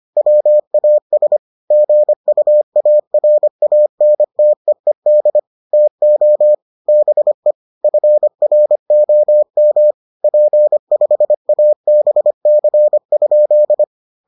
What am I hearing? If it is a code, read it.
Morse code, heard as a keyed tone, about 25 wpm, 600 Hz. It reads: WAS GUARANTEED TO BE FROM P5ABC?